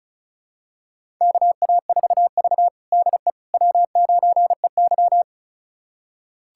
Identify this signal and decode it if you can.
Morse code, heard as a keyed tone, about 36 wpm, 700 Hz. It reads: KA4V DE W9EY
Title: USA Callsign DE Callsign - 4 Characters - 35wpm